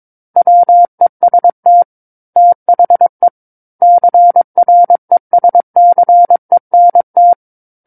Morse code, heard as a keyed tone, about 22 wpm, 700 Hz. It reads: WEST THE CRESCENT
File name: Book One / 13-Chapter 13 - 22 WPM